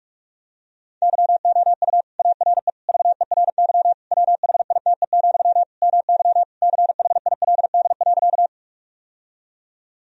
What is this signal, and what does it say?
Morse code, heard as a keyed tone, about 41 wpm, 700 Hz. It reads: YOU ARE VERY WHITE, MY CHILD.
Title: Sentences from Top 1000 Words - Review Part 1 - 45wpm